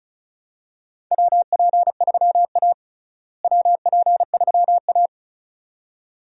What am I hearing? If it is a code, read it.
Morse code, heard as a keyed tone, about 34 wpm, 700 Hz. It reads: WP3A WP3A
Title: USA Call Signs - 4 Characters - Repeated 2x - 35wpm